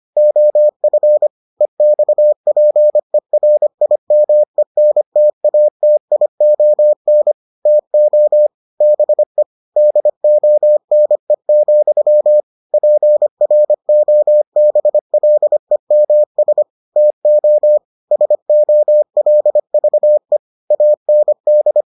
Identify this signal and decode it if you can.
Morse code, heard as a keyed tone, about 25 wpm, 600 Hz. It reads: OF EXPERIMENTATION TO BE DONE, PROBLEMS TO SOLVE AND